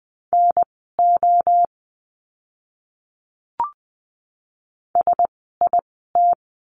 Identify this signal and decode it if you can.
Morse code, heard as a keyed tone, about 20 wpm, spaced as for 14 wpm, 700 Hz. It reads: NO SIT
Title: 036 Lesson 7f - Letter s - Words 20wpm